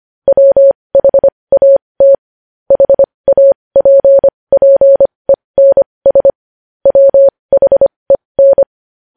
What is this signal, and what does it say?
Morse code, heard as a keyed tone, about 25 wpm, spaced as for 20 wpm, 550 Hz. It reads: WHAT HAPPENS WHEN
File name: podcast / News headlines-1x-25x20